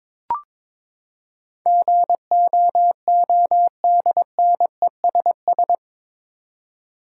Morse code, heard as a keyed tone, about 22 wpm, 700 Hz. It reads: GOODNESS
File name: practice / 22wpm / Eight Letter Words - 22wpm